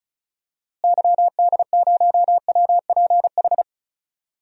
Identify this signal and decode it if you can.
Morse code, heard as a keyed tone, about 36 wpm, 700 Hz. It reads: YD0WPH
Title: DXCC-100 Call Signs - 6 Characters - Repeated 3x - 35wpm